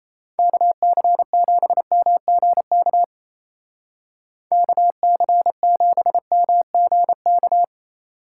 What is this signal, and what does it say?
Morse code, heard as a keyed tone, about 37 wpm, 700 Hz. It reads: KC7MGK KC7MGK